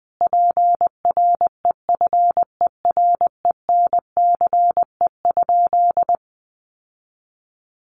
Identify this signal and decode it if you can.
Morse code, heard as a keyed tone, about 19 wpm, 700 Hz. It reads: PREFERENCE?
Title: Sentences from Top 1500 Words 20wpm